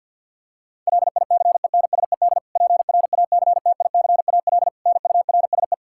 Morse code, heard as a keyed tone, about 53 wpm, 700 Hz. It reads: LICENSED PRACTICAL NURSE
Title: Practice QSO - Exchange 3 - More Info - 50wpm